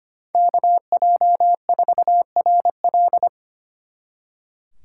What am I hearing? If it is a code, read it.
Morse code, heard as a keyed tone, about 30 wpm, 700 Hz. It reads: KJ4RL
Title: Speed Racing - USA 2x2 Call Signs 25wpm